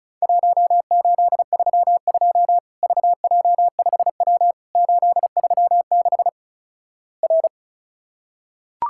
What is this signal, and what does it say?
Morse code, heard as a keyed tone, about 36 wpm, 700 Hz. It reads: 1832 VJ5W 836 R
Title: Worked All Europe DX Contest - QTC Exchange - 35wpm